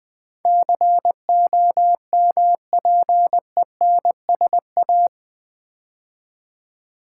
Morse code, heard as a keyed tone, about 20 wpm, 700 Hz. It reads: COMPENSA
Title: Build A Word - 4 Letters At A Time - 10-Letter Words - 20wpm